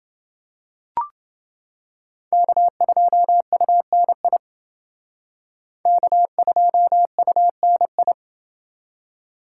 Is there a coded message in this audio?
Yes: Morse code from 2.3 to 8.1 s, heard as a keyed tone, about 28 wpm, 700 Hz. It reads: K2UNI K2UNI